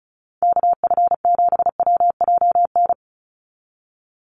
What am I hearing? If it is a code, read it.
Morse code, heard as a keyed tone, about 34 wpm, 700 Hz. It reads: KF7WJN
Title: USA Call Signs 35wpm